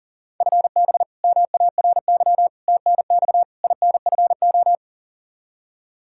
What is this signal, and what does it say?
Morse code, heard as a keyed tone, about 40 wpm, 700 Hz. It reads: FB MARY TNX INFO
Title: Practice QSO - Exchange 3 - More Info - 40wpm